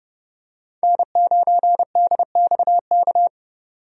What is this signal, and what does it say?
Morse code, heard as a keyed tone, about 32 wpm, 700 Hz. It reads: N9DXK
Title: USA Call Signs - 5 Characters - 30wpm